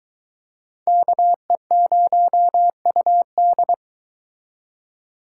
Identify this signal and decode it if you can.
Morse code, heard as a keyed tone, about 23 wpm, 700 Hz. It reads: KE0UD